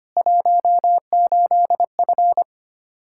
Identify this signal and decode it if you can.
Morse code, heard as a keyed tone, about 25 wpm, 700 Hz. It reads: 18F
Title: Practice QSO - Exchange 2 - Info - 25wpm